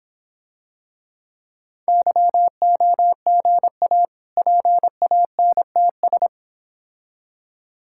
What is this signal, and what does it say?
Morse code, heard as a keyed tone, about 26 wpm, 700 Hz. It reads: YOGA PANTS